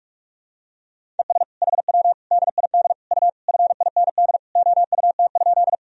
Morse code, heard as a keyed tone, about 46 wpm, 700 Hz. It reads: ES HW DID U FIND OUT?
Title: Sentences from Top 100 Words and Abbreviations 45wpm